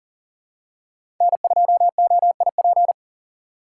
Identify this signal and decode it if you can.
Morse code, heard as a keyed tone, about 40 wpm, 700 Hz. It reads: N2OIP
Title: USA Call Signs - 3x Character Spacing - 40wpm